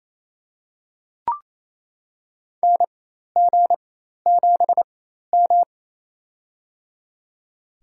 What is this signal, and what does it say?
Morse code, heard as a keyed tone, about 27 wpm, spaced as for 13 wpm, 700 Hz. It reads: NG7M
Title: USA Call Signs - 4x Character Spacing - 28wpm